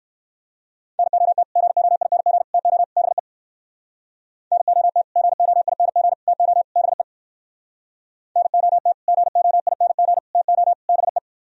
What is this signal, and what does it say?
Morse code, heard as a keyed tone, about 58 wpm, 700 Hz. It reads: NOT GOING TO BE NOT GOING TO BE NOT GOING TO BE